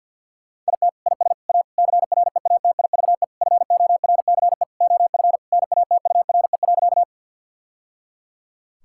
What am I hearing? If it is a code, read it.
Morse code, heard as a keyed tone, about 50 wpm, 700 Hz. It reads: IT IS A CREATIVE FORCE OF NATURE.
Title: Sentences from Top 1000 Words - Review Part 5 - 50wpm